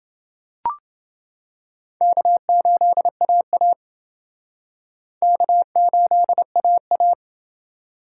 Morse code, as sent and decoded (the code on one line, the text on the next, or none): -.- ---.. .- .- / -.- ---.. .- .-
K8AA K8AA